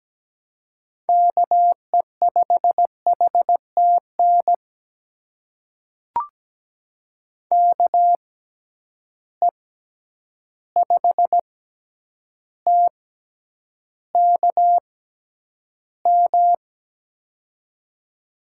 Morse code, heard as a keyed tone, about 17 wpm, spaced as for 9 wpm, 700 Hz. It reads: KE5HTN KE5TKM
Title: USA Call Signs - 6 Characters - 6x Character Spacing - 17wpm